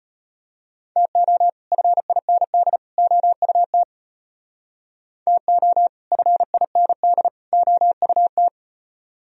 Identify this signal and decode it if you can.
Morse code, heard as a keyed tone, about 33 wpm, 700 Hz. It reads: TO FIND OUT TO FIND OUT